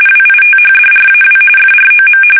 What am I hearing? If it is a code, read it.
RTTY, about 45.45 baud. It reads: WITHDRAW FROM
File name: fsk850 / battleshipnj